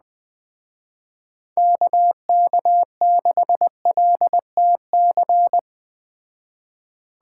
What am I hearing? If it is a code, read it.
Morse code, heard as a keyed tone, about 19 wpm, 700 Hz. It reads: KK6LTC
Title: Speed Racing - USA 2x3 Call Signs 17wpm